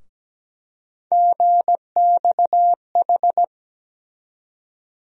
Morse code, heard as a keyed tone, about 17 wpm, 700 Hz. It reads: GXH
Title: USA Call Sign Postfixes - 3 Letters 17wpm